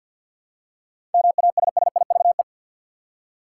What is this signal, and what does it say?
Morse code, heard as a keyed tone, about 51 wpm, 700 Hz. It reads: MASSIVE
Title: Seven Letter Words - 50wpm